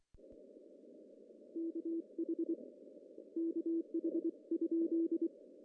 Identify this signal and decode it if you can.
Morse code, heard as a keyed tone, about 23 wpm, 330 Hz. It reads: KH KH?
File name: CW / cqww-cw-k5tr